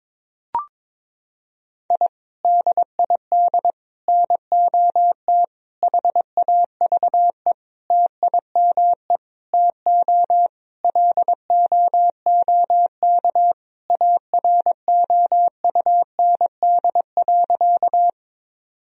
Morse code, heard as a keyed tone, about 22 wpm, 700 Hz. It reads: I DID NOT HAVE TIME TO LOOK AROUND.
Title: Sentences from Top 1000 Words - Review Part 3 - 22wpm